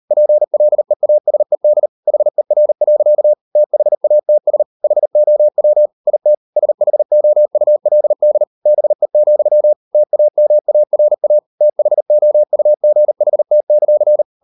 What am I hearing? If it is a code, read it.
Morse code, heard as a keyed tone, about 40 wpm, 600 Hz. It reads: PLEASED HER. THATS HOW IT SHOULD BE, TAMARA THOUGHT;